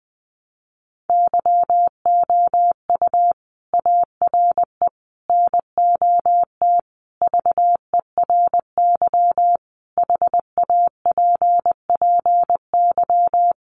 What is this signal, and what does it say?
Morse code, heard as a keyed tone, about 20 wpm, 700 Hz. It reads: YOU ARE NOT VERY HAPPY
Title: Sentences from Top 1200 Words 20wpm